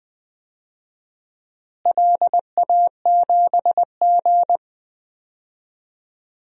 Morse code, heard as a keyed tone, about 20 wpm, 700 Hz. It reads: LA7G
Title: DXCC-100 Call Signs - 4 Characters - Repeated 3x - 20wpm